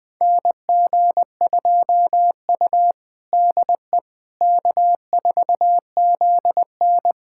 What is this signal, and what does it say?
Morse code, heard as a keyed tone, about 20 wpm, 700 Hz. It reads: NG2U DE K4ZN
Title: USA Callsign DE Callsign - 4 Characters - 20wpm